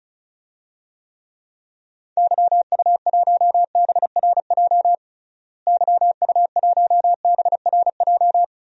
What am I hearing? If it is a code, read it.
Morse code, heard as a keyed tone, about 34 wpm, 700 Hz. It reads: YU1BRJ YU1BRJ
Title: DXCC-100 Call Signs - 6 Characters - Repeated 2x - 35wpm